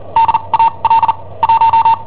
Morse code, heard as a keyed tone, about 39 wpm, spaced as for 30 wpm, 910 Hz. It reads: DAL 1
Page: Cassetta esercizi preparata da I7OHP per IW5ELP